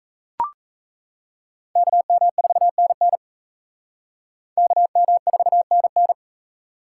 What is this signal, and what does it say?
Morse code, heard as a keyed tone, about 37 wpm, 700 Hz. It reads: KM4NN KM4NN